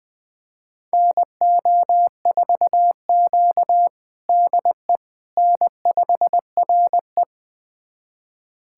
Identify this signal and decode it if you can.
Morse code, heard as a keyed tone, about 21 wpm, 700 Hz. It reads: NO4Q DE N5RE
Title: USA Callsign DE Callsign - 4 Characters - 20wpm